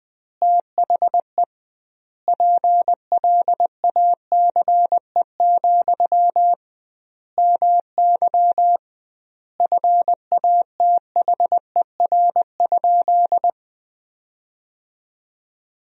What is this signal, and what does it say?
Morse code, heard as a keyed tone, about 20 wpm, 700 Hz. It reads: THE PLACE, MY FATHER?
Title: 2x Word-Spacing - Sentences from Top 400 Words - 20wpm